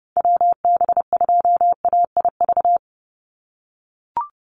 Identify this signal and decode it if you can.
Morse code, heard as a keyed tone, about 30 wpm, 700 Hz. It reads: WB2AIV
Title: Call Signs - 30wpm